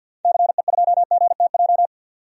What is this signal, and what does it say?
Morse code, heard as a keyed tone, about 52 wpm, 700 Hz. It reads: CE2GTJ
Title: DXCC-100 Call Signs - 6 Characters - 50wpm